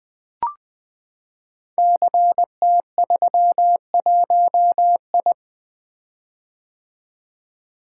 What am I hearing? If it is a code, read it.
Morse code, heard as a keyed tone, about 21 wpm, 700 Hz. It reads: CT31I
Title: Code Groups - 5 Letter:Numbers - 20wpm